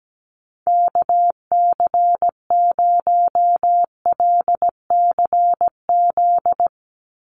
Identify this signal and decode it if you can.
Morse code, heard as a keyed tone, about 17 wpm, 700 Hz. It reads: KC0LCZ
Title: USA Call Signs 17wpm